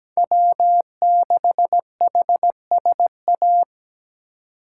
Morse code, heard as a keyed tone, about 17 wpm, 700 Hz. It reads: W6HSA
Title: USA Call Signs - 5 Characters - 2x Character Spacing - 17wpm